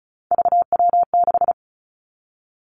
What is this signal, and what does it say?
Morse code, heard as a keyed tone, about 36 wpm, 700 Hz. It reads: VW6
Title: Three Challenging Letter-Number Combinations 35wpm